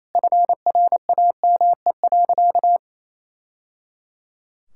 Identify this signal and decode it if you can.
Morse code, heard as a keyed tone, about 28 wpm, 700 Hz. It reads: FRAME.
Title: Sentences from Top 1000 Words - Review Part 6 - 28wpm